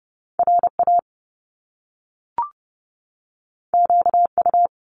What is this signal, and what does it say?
Morse code, heard as a keyed tone, about 30 wpm, 700 Hz. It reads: RA QU
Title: Top Two Letter Combinations in Words 30wpm